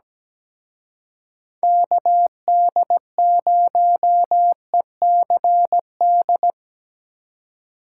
Morse code, heard as a keyed tone, about 19 wpm, 700 Hz. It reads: KD0ECD
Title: Speed Racing - USA 2x3 Call Signs 17wpm